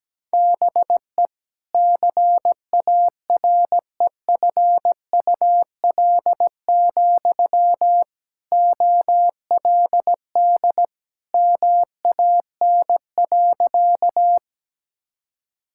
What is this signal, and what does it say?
Morse code, heard as a keyed tone, about 17 wpm, 700 Hz. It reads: BE CAREFUL, OLD MAN.
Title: Sentences from Top 1600 Words 17wpm